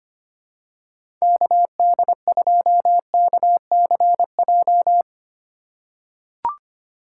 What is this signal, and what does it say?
Morse code, heard as a keyed tone, about 25 wpm, 700 Hz. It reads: KD2KCJ
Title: USA Call Signs - 4x Character Spacing - 25wpm